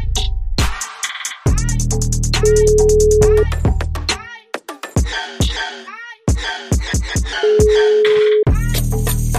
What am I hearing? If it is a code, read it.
Ringback tone (Europe, 425 Hz).